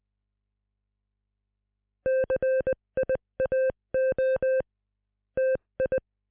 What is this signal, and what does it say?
Morse code, heard as a keyed tone, about 20 wpm, 520 Hz. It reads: CIAO TI